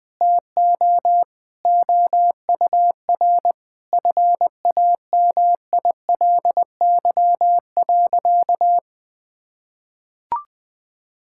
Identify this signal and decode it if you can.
Morse code, heard as a keyed tone, about 21 wpm, 700 Hz. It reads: TO OUR FAMILY.
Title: Sentences from Top 1000 Words - Review Part 6 - 20wpm